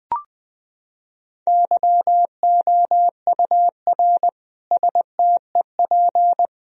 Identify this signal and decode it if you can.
Morse code, heard as a keyed tone, about 20 wpm, 700 Hz. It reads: YOUR STEP
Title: Sentences from Top 1000 Words - Review Part 2 - 20wpm